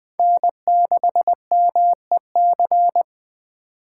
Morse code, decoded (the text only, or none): N6MEC